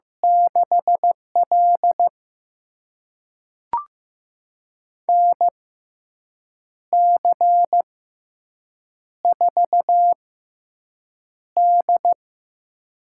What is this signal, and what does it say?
Morse code, heard as a keyed tone, about 15 wpm, spaced as for 6 wpm, 700 Hz. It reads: 6L NC4D
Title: USA Call Signs - 4 Characters - 6x Character Spacing - 15wpm